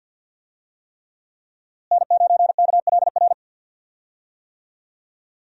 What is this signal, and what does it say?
Morse code, heard as a keyed tone, about 52 wpm, 700 Hz. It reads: N9KLR
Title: USA Call Signs - 5 Characters - Repeated 3x - 50wpm